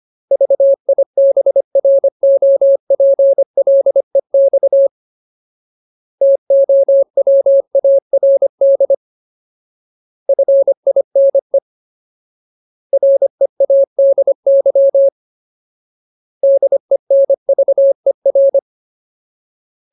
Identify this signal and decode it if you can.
Morse code, heard as a keyed tone, about 25 wpm, 550 Hz. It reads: VIBROPLEX TOWARD FINE READY DENVER
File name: Ultimate Word List / 25WPM Ultimate